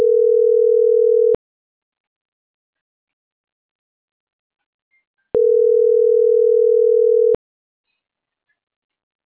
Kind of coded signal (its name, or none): telephone call progress tone